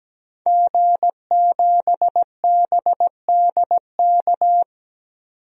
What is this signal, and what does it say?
Morse code, heard as a keyed tone, about 17 wpm, 700 Hz. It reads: G7BDK
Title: DXCC-100 Call Signs - 5 Characters - 17wpm